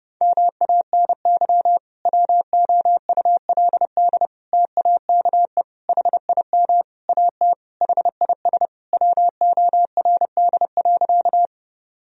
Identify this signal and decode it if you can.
Morse code, heard as a keyed tone, about 32 wpm, 700 Hz. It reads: MANY WOULD TAKE HIM AT HIS WORD.
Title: Sentences from Top 1000 Words - Review Part 1 - 30wpm